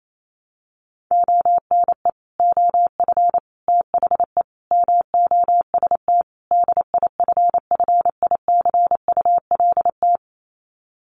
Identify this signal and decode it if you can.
Morse code, heard as a keyed tone, about 28 wpm, 700 Hz. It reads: ONE OF THE MOST DIFFICULT
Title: Sets of 5 Words Encore2 28wpm